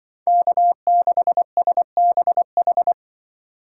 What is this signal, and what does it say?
Morse code, heard as a keyed tone, about 23 wpm, 700 Hz. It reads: K6SBH
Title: Speed Racing - USA Call Signs 20wpm